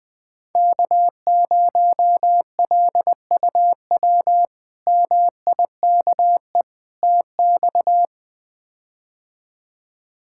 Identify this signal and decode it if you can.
Morse code, heard as a keyed tone, about 20 wpm, 700 Hz. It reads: K0LUW MIKE TX
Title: K1USN SST QSO Practice - 20wpm